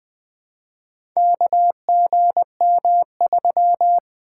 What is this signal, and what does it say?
Morse code, heard as a keyed tone, about 20 wpm, 700 Hz. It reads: KGM3
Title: Four Letter-Number Combinations - 20wpm